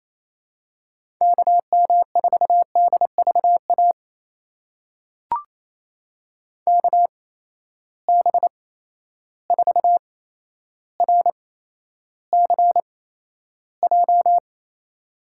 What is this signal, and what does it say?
Morse code, heard as a keyed tone, about 28 wpm, spaced as for 12 wpm, 700 Hz. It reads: KM4DVA KB4RCJ
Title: USA Call Signs - 6 Characters - 8x Character Spacing - 28wpm